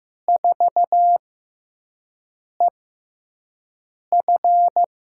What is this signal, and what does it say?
Morse code, heard as a keyed tone, about 15 wpm, spaced as for 5 wpm, 700 Hz. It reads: 4EF
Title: USA Call Signs - 4 Characters - 6x Character Spacing - 15wpm